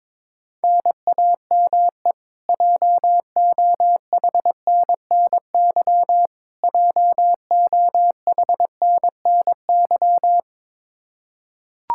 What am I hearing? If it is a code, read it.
Morse code, heard as a keyed tone, about 22 wpm, 700 Hz. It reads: NAME JOHNNY JOHNNY
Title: Practice QSO - Exchange 1 - Beginning - 22wpm